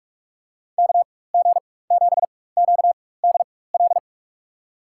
Morse code, heard as a keyed tone, about 45 wpm, spaced as for 21 wpm, 700 Hz. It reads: KG7QDL
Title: USA Call Signs - 6 Characters - 4x Character Spacing - 45wpm